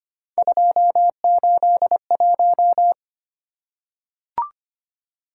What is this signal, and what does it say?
Morse code, heard as a keyed tone, about 25 wpm, 700 Hz. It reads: 281
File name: Three Letter-Number Combinations - 25wpm